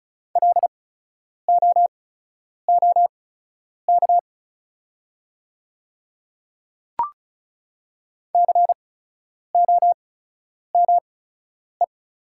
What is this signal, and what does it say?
Morse code, heard as a keyed tone, about 36 wpm, spaced as for 10 wpm, 700 Hz. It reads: LOOK COME
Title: 8x Character Spacing - Top 100 Words - 35wpm